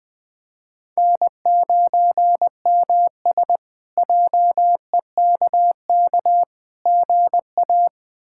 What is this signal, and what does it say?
Morse code, heard as a keyed tone, about 20 wpm, 700 Hz. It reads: N9MS JEKK GA
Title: K1USN SST QSO Practice - 20wpm